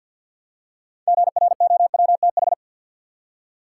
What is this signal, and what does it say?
Morse code, heard as a keyed tone, about 53 wpm, 700 Hz. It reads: GROWTH
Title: Six Letter Words - 50wpm